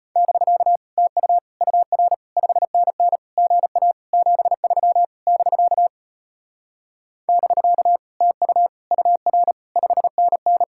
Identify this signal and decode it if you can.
Morse code, heard as a keyed tone, about 33 wpm, 700 Hz. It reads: <BK> TU UR 5NN GA 73 <BK> <BK> TU UR 5NN